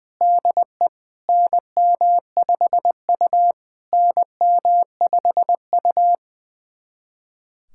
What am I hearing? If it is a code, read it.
Morse code, heard as a keyed tone, about 20 wpm, 700 Hz. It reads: DE NM5U NM5U
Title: SOTA QSO Practice 20wpm